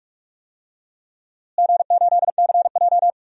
Morse code, heard as a keyed tone, about 46 wpm, 700 Hz. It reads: G8CJ
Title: DXCC-100 Call Signs - 4 Characters - Repeated 3x - 45wpm